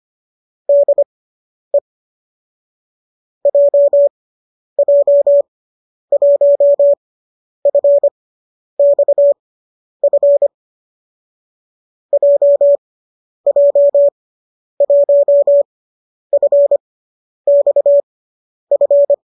Morse code, heard as a keyed tone, about 25 wpm, spaced as for 10 wpm, 570 Hz. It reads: DE JJ1FXF JJ1FXF